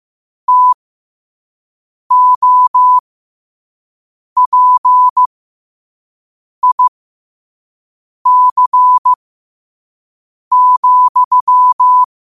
Morse code, heard as a keyed tone, about 15 wpm, spaced as for 5 wpm, 1000 Hz. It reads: TOPIC,
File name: qotdfiles / 5wpm